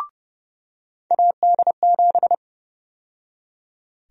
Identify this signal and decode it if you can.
Morse code, heard as a keyed tone, about 30 wpm, 700 Hz. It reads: AD7
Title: USA Call Sign Prefixes - 30wpm